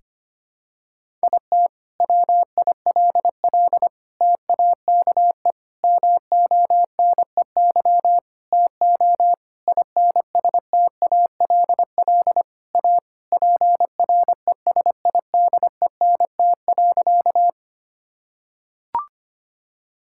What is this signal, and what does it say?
Morse code, heard as a keyed tone, about 25 wpm, 700 Hz. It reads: IT WILL TAKE MONEY TO INSTALL A PRESIDENT.
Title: Sentences from Top 1600 Words 25wpm